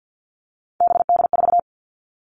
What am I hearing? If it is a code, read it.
Morse code, heard as a keyed tone, about 50 wpm, 690 Hz. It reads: BD4
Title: Callsign Prefixes for Top 100 DXCC - 50wpm